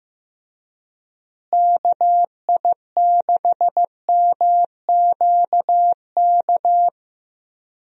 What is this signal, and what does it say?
Morse code, heard as a keyed tone, about 15 wpm, 700 Hz. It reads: KI6MQK
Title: USA Call Signs - 6 Characters - Repeated 3x - 15wpm